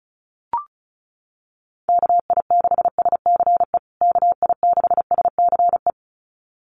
Morse code, heard as a keyed tone, about 34 wpm, 700 Hz. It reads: KI6SCE KI6SCE
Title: USA Call Signs Repeated 2x 35wpm